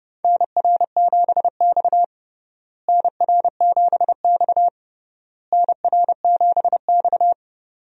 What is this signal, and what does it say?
Morse code, heard as a keyed tone, about 30 wpm, 700 Hz. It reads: NR7X NR7X NR7X
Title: USA Call Signs - 4 Characters - Repeated 3x - 30wpm